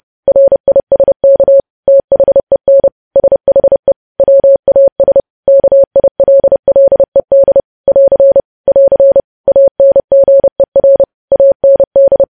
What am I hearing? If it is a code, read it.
Morse code, heard as a keyed tone, about 30 wpm, 550 Hz. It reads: RISK THEN SHE WAS KILLED <AR> <AR> ANGER AND
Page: news headlines in morse code at 30 wpm